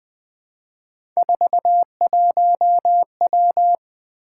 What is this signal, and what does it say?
Morse code, heard as a keyed tone, about 20 wpm, 700 Hz. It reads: 41W
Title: Three Letter-Number Combinations - 20wpm